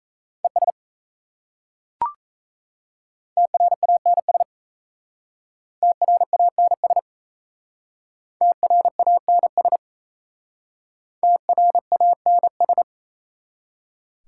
Morse code, heard as a keyed tone, about 34 wpm, 700 Hz. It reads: ES TRANS TRANS TRANS TRANS
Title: Speed-Racing Prefixes and Suffixes 28wpm